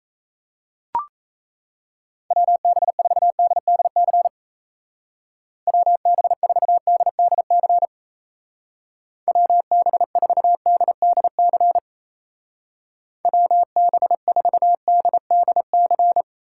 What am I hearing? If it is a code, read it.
Morse code, heard as a keyed tone, about 34 wpm, 700 Hz. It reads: WB4DDC WB4DDC WB4DDC WB4DDC
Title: Speed Racing - USA 2x3 Call Signs 28wpm